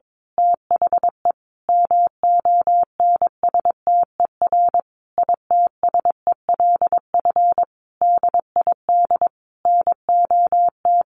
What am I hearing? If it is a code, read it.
Morse code, heard as a keyed tone, about 22 wpm, 700 Hz. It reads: THE MONSTER ITSELF DID NOT
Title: A Sound Of Thunder 22wpm